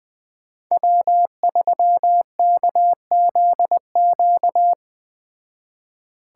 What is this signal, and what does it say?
Morse code, heard as a keyed tone, about 20 wpm, 700 Hz. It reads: W3KZQ
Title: Speed Racing - USA 1x3 Call Signs 17wpm